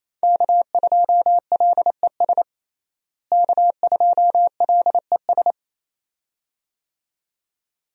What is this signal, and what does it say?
Morse code, heard as a keyed tone, about 28 wpm, 700 Hz. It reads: K2LES K2LES
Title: USA Call Signs - 5 Characters - Repeated 2x - 28wpm